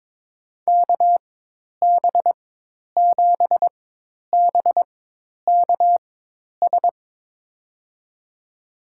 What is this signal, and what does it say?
Morse code, heard as a keyed tone, about 22 wpm, spaced as for 10 wpm, 700 Hz. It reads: KB7BKS
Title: USA Call Signs - 6 Characters - 4x Character Spacing - 22wpm